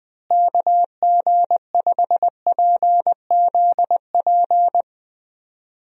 Morse code, heard as a keyed tone, about 20 wpm, 700 Hz. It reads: KG5PZP
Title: USA Call Signs - 6 Characters - Repeated 3x - 20wpm